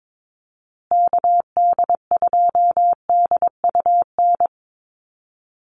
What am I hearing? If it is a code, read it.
Morse code, heard as a keyed tone, about 22 wpm, 700 Hz. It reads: KD2DUN
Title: USA Call Signs Repeated 2x 22wpm